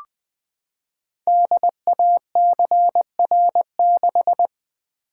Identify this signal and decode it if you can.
Morse code, heard as a keyed tone, about 20 wpm, 700 Hz. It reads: DACR6
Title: Code Groups - 5 Letter:Numbers - 20wpm